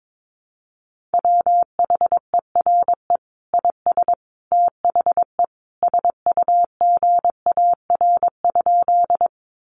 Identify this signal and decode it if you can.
Morse code, heard as a keyed tone, about 22 wpm, 700 Hz. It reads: WHERE IS THE SUGAR?
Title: Sentences from Top 1200 Words 22wpm